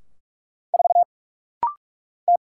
Morse code, heard as a keyed tone, about 45 wpm, 700 Hz. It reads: 4 T
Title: Single Letter-Number - Rapid-Fire 45wpm